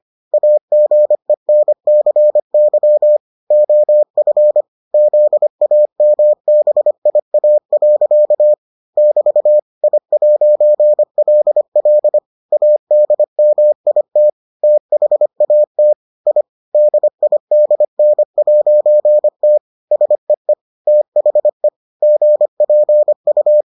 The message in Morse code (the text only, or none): AGENCY OF ZAMBIA. <BT> I'LL ADMIT THAT I DIDN'T SEE THE GPU